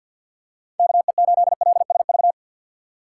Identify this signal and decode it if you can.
Morse code, heard as a keyed tone, about 52 wpm, 690 Hz. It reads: KE8LSV
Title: USA Call Signs - 6 Characters - 3x Character Spacing - 50wpm